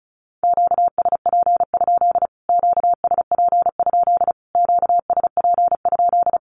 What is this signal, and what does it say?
Morse code, heard as a keyed tone, about 34 wpm, 700 Hz. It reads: QSP? QSP? QSP?
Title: Q-Codes Repeated 3x 35wpm